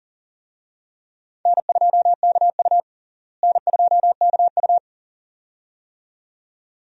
Morse code, heard as a keyed tone, about 40 wpm, 700 Hz. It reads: N2KU N2KU
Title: USA Call Signs - 4 Characters - Repeated 2x - 40wpm